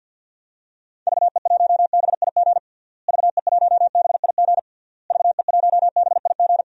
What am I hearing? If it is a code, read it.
Morse code, heard as a keyed tone, about 46 wpm, 700 Hz. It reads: VE1BIG VE1BIG VE1BIG
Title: DXCC-100 Call Signs - 6 Characters - Repeated 3x - 50wpm